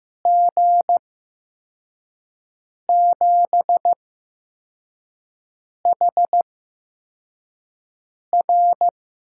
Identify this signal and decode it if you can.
Morse code, heard as a keyed tone, about 15 wpm, spaced as for 4 wpm, 700 Hz. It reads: G7HR
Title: USA Call Signs - 6 Characters - 8x Character Spacing - 15wpm